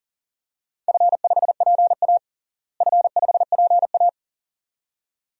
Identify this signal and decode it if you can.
Morse code, heard as a keyed tone, about 40 wpm, 700 Hz. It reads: F5PA F5PA
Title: DXCC-100 Call Signs - 4 Characters - Repeated 2x - 40wpm